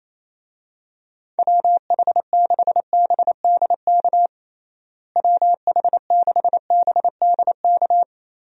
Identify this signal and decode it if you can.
Morse code, heard as a keyed tone, about 28 wpm, 700 Hz. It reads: WH6BDK WH6BDK
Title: USA Call Signs - 6 Characters - Repeated 2x - 28wpm